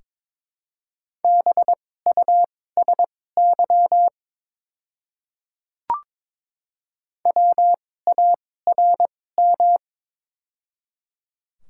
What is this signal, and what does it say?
Morse code, heard as a keyed tone, about 22 wpm, spaced as for 16 wpm, 690 Hz. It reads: BUSY WARM
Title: 2x Character Spacing - Top 500 Words - 22wpm